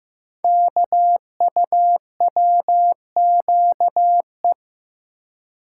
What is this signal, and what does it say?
Morse code, heard as a keyed tone, about 15 wpm, 700 Hz. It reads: KUWQE
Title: Code Groups - 5 Letters - 15wpm